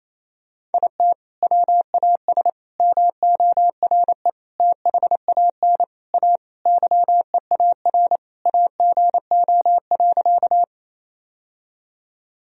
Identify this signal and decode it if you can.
Morse code, heard as a keyed tone, about 27 wpm, 700 Hz. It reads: IT WAS MORE THAN A YEAR AGO.
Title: Sentences from Top 1000 Words - Review Part 3 - 28wpm